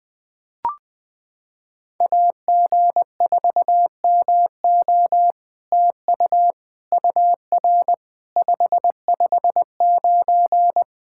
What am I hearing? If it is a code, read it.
Morse code, heard as a keyed tone, about 20 wpm, 700 Hz. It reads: AG4MO TU UR 559
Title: SOTA QSO Practice 20wpm